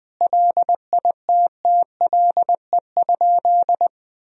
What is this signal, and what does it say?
Morse code, heard as a keyed tone, about 20 wpm, 700 Hz. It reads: LITTLE?
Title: Sentences from Top 500 Words and Abbreviations 20wpm